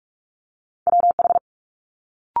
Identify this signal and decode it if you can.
Morse code, heard as a keyed tone, about 46 wpm, 700 Hz. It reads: WH
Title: Multi-letter Phonemes Quick Fire 45wpm